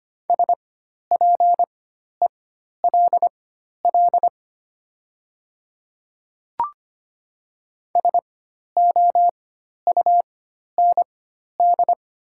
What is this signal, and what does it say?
Morse code, heard as a keyed tone, about 25 wpm, spaced as for 12 wpm, 690 Hz. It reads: SPELL SOUND
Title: 4x Character Spacing - Top 200 Words - 25wpm